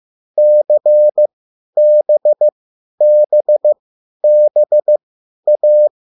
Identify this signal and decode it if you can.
Morse code, heard as a keyed tone, about 15 wpm, spaced as for 10 wpm, 600 Hz. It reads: CBBBA